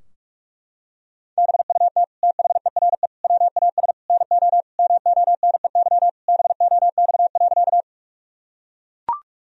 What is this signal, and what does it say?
Morse code, heard as a keyed tone, about 42 wpm, 700 Hz. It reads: BUT THERE WAS NO MONEY BOX.
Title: Sentences from Top 1000 Words - Review Part 2 - 45wpm